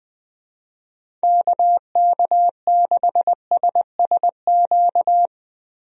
Morse code, heard as a keyed tone, about 20 wpm, 700 Hz. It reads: KK6SSQ
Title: USA Call Signs - 6 Characters - 20wpm